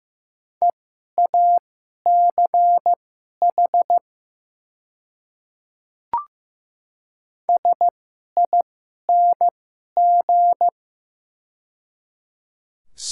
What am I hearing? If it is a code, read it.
Morse code, heard as a keyed tone, about 15 wpm, spaced as for 11 wpm, 700 Hz. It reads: EACH SING
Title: 2x Character Spacing - Top 500 Words - 15wpm